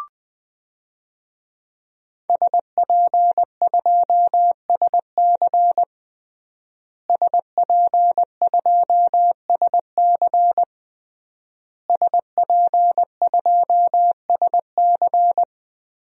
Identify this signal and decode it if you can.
Morse code, heard as a keyed tone, about 20 wpm, 700 Hz. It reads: SP2SC SP2SC SP2SC
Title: DXCC-100 Call Signs - 5 Characters - Repeated 3x - 20wpm